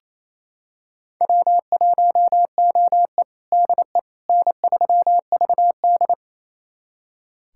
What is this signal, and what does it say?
Morse code, heard as a keyed tone, about 28 wpm, 700 Hz. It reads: W1OE DE N3VD
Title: USA Callsign DE Callsign - 4 Characters - 28wpm